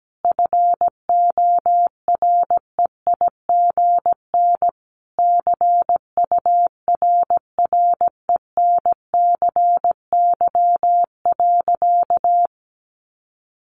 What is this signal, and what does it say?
Morse code, heard as a keyed tone, about 17 wpm, 700 Hz. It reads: FOREIGN CURRENCY.
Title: Sentences from Top 1500 Words 17wpm